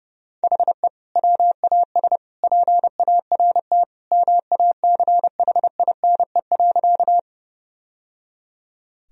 Morse code, heard as a keyed tone, about 30 wpm, 700 Hz. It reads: HE WAS PART MACHINE.
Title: Sentences from Top 1000 Words - Review Part 2 - 30wpm